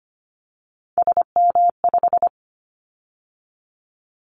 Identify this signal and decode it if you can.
Morse code, heard as a keyed tone, about 25 wpm, 700 Hz. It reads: SM5
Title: Callsign Prefixes for Top 25 DXCC - 25wpm